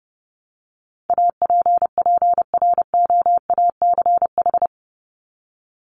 Morse code, heard as a keyed tone, about 30 wpm, 700 Hz. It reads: APPROACH
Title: Top 751-1000 Words 30wpm